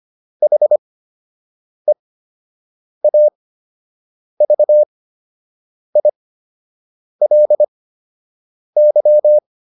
Morse code, heard as a keyed tone, about 25 wpm, spaced as for 7 wpm, 600 Hz. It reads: HEAVILY